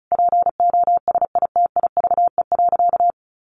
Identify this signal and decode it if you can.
Morse code, heard as a keyed tone, about 34 wpm, 700 Hz. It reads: POSITIVE.
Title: Sentences from Top 1400 Words 35wpm